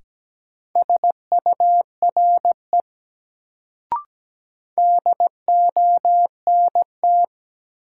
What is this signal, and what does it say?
Morse code, heard as a keyed tone, about 17 wpm, 700 Hz. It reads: SURE DONT
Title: Sentences from Top 300 Words - New Word Drill - 17wpm